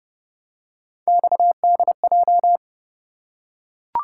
Morse code, heard as a keyed tone, about 26 wpm, 700 Hz. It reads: XDJ